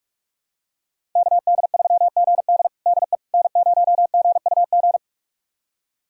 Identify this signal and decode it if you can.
Morse code, heard as a keyed tone, about 46 wpm, 700 Hz. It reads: KD3GD DE N0GUG
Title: USA Callsign DE Callsign - 5 Characters - 45wpm